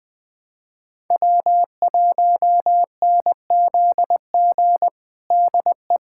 Morse code, heard as a keyed tone, about 20 wpm, 700 Hz. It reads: W1NZG DE
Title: USA Callsign DE Callsign 20wpm